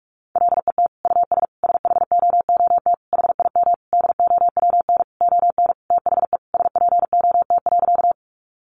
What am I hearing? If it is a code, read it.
Morse code, heard as a keyed tone, about 44 wpm, 700 Hz. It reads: LET US SHOOT HIM DOWN ON THE SPOT.
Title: Sentences from Top 1200 Words 45wpm